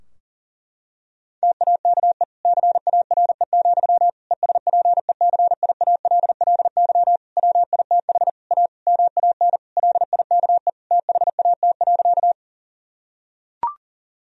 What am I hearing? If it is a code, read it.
Morse code, heard as a keyed tone, about 40 wpm, 700 Hz. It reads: TAKE CARE, ESPECIALLY WITH A MAN LIKE THAT.
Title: Sentences from Top 1000 Words - Review Part 4 - 40wpm